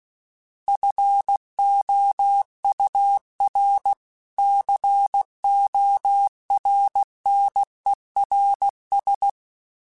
Morse code, heard as a keyed tone, about 16 wpm, 800 Hz. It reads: FOUR CORNERS
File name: fromSMs / FromSMs mp3s / Cities@15